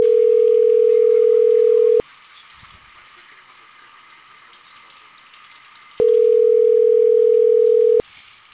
Ringback tone (North America), 440 + 480 Hz.